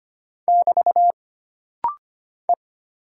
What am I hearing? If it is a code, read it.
Morse code, heard as a keyed tone, about 25 wpm, 700 Hz. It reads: <BT> E
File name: Single Character (Letter:Number:Punctuation:Prosign) - Rapid-Fire - 25wpm